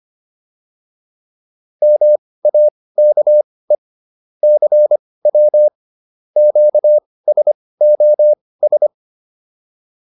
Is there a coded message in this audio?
Yes: Morse code from 1.8 to 8.9 s, heard as a keyed tone, about 25 wpm, spaced as for 18 wpm, 600 Hz. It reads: MAKE CW QSOS